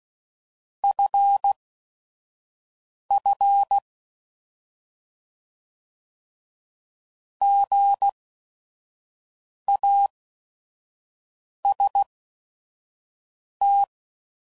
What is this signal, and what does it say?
Morse code, heard as a keyed tone, about 16 wpm, spaced as for 5 wpm, 800 Hz. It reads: FF GAST
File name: Prefixes / By First Letter / Gs